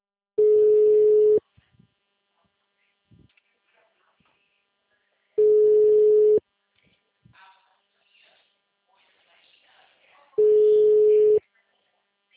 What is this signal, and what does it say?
Ringback tone (Europe, 425 Hz), with speech.